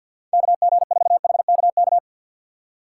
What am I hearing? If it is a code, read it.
Morse code, heard as a keyed tone, about 65 wpm, spaced as for 57 wpm, 700 Hz. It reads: KG4HKK
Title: Speed Racing - USA 2x3 Call Signs 50wpm